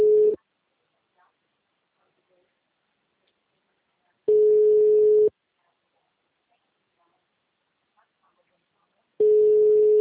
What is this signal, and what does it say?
Ringback tone (Europe, 425 Hz).